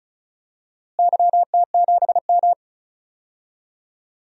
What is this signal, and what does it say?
Morse code, heard as a keyed tone, about 36 wpm, 700 Hz. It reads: YT7M
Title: DXCC-100 Call Signs - 4 Characters - Repeated 3x - 35wpm